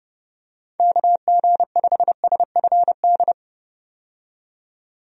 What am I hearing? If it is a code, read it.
Morse code, heard as a keyed tone, about 30 wpm, 700 Hz. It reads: KG5SFD
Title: USA Call Signs - 6 Characters - 30wpm